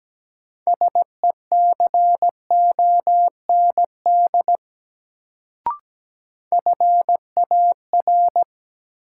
Morse code, heard as a keyed tone, about 17 wpm, 700 Hz. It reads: SECOND FAR
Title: Sentences from Top 300 Words - New Word Drill - 17wpm